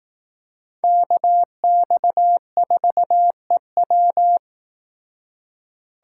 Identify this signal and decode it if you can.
Morse code, heard as a keyed tone, about 18 wpm, 700 Hz. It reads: KX4EW
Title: Speed Racing - USA 2x2 Call Signs 15wpm